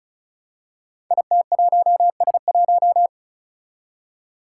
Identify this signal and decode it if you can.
Morse code, heard as a keyed tone, about 36 wpm, 700 Hz. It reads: IT1S1
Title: Five Challenging Letter-Number Combinations 35wpm